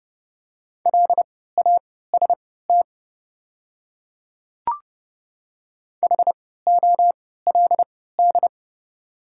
Morse code, heard as a keyed tone, about 30 wpm, spaced as for 17 wpm, 700 Hz. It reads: LAST HOLD
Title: 3x Character Spacing - Top 500 Words - 30wpm